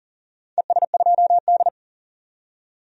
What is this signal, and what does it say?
Morse code, heard as a keyed tone, about 40 wpm, 700 Hz. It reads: ES2D
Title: DXCC-100 Call Signs - 4 Characters - 40wpm